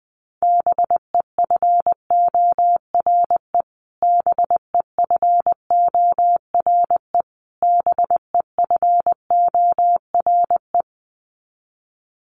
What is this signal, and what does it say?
Morse code, heard as a keyed tone, about 20 wpm, 700 Hz. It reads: BEFORE BEFORE BEFORE
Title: Top 500 Words Repeated 20wpm